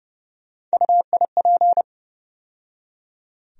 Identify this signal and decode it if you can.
Morse code, heard as a keyed tone, about 32 wpm, 700 Hz. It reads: UIP
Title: USA Call Sign Postfixes - 3 Letters 30wpm